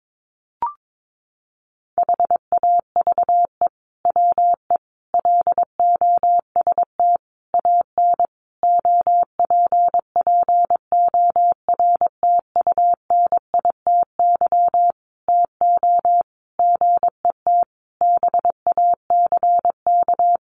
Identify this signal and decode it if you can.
Morse code, heard as a keyed tone, about 22 wpm, 700 Hz. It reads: HAVE WE LOST AN OPPORTUNITY TO GET BACK
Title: Sentences from Top 1400 Words 22wpm